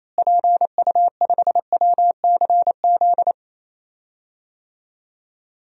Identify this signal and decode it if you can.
Morse code, heard as a keyed tone, about 28 wpm, 700 Hz. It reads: PU5WCZ
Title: DXCC-100 Call Signs - 6 Characters - Repeated 3x - 28wpm